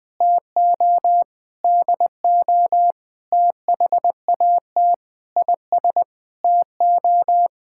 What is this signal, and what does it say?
Morse code, heard as a keyed tone, about 20 wpm, 700 Hz. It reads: TO DO THAT IS TO
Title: Sentences with 9-Word Phrases - Spoken Beginning - 20wpm